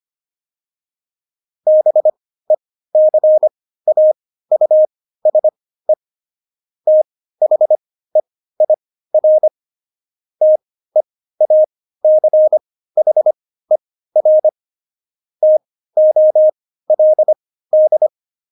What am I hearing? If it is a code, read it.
Morse code, heard as a keyed tone, about 25 wpm, spaced as for 15 wpm, 620 Hz. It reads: BECAUSE THEIR TEACHER TOLD
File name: Jokes-25-char-15-eff